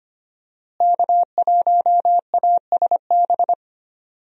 Morse code, heard as a keyed tone, about 25 wpm, 700 Hz. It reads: K1ASB
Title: USA Call Signs - 5 Characters - Repeated 3x - 25wpm